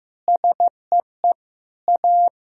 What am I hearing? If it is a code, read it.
Morse code, heard as a keyed tone, about 15 wpm, 700 Hz. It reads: SEE A
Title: Sentences from Top 1000 Words - Review Part 4 - 15wpm